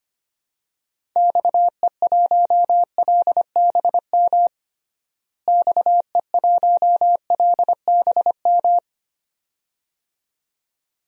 Morse code, heard as a keyed tone, about 25 wpm, 700 Hz. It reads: XE1LBM XE1LBM
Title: DXCC-100 Call Signs - 6 Characters - Repeated 2x - 25wpm